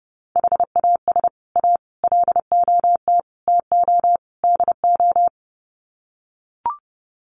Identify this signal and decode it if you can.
Morse code, heard as a keyed tone, about 29 wpm, 700 Hz. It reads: HAS A LOT TO DO
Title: Sets of 5 Words Encore2 30wpm